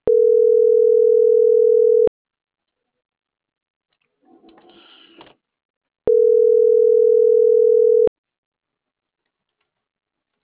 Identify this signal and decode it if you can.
Ringback tone (North America), 440 + 480 Hz.